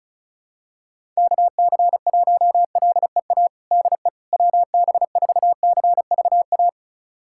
Morse code, heard as a keyed tone, about 36 wpm, 700 Hz. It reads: KC1LEA DE WB4CVA
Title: USA Callsign DE Callsign 35wpm